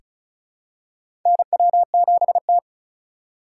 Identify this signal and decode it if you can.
Morse code, heard as a keyed tone, about 36 wpm, 700 Hz. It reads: NW7T
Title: USA Call Signs - 4 Characters - 3x Character Spacing - 35wpm